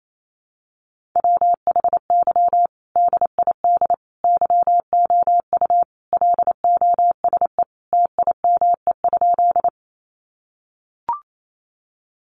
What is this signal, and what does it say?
Morse code, heard as a keyed tone, about 28 wpm, 700 Hz. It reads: WHY DID YOU LOSE TIME?
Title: Sentences from Top 1100 Words 28wpm